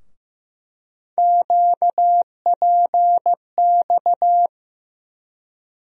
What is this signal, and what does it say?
Morse code, heard as a keyed tone, about 15 wpm, 700 Hz. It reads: QPX
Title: USA Call Sign Postfixes - 3 Letters 15wpm